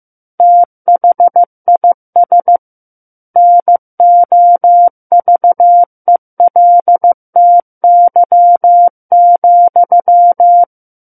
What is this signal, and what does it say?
Morse code, heard as a keyed tone, about 15 wpm, 700 Hz. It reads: THIS NOVELTY,
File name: Chapter 6 - 15 WPM